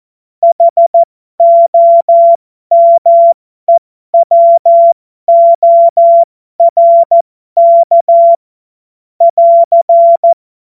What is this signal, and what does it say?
Morse code, heard as a keyed tone, about 14 wpm, spaced as for 12 wpm, 680 Hz. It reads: HOMEWORK <AR>
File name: Morsekurs / Witze